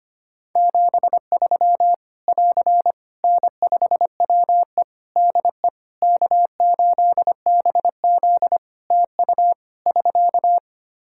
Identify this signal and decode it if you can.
Morse code, heard as a keyed tone, about 25 wpm, 700 Hz. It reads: 73 <AR> N5WE DE K8BZ TU <SK>
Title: K1USN SST QSO Practice 25wpm